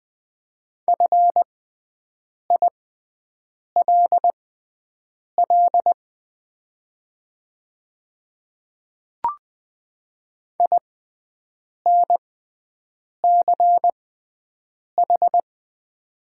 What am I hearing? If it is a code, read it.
Morse code, heard as a keyed tone, about 20 wpm, spaced as for 7 wpm, 700 Hz. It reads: FILL INCH
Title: 6x Character Spacing - Top 500 Words - 20wpm